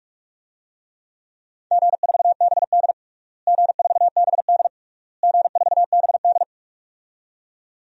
Morse code, heard as a keyed tone, about 45 wpm, 700 Hz. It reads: G4BD G4BD G4BD
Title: DXCC-100 Call Signs - 4 Characters - Repeated 3x - 45wpm